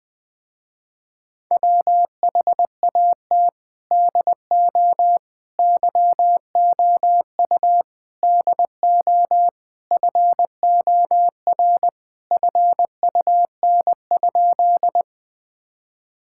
Morse code, heard as a keyed tone, about 20 wpm, 700 Hz. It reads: WHAT DO YOU DO FOR FUN?
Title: Sentences with 4-Word Phrases - Spoken Beginning - 20wpm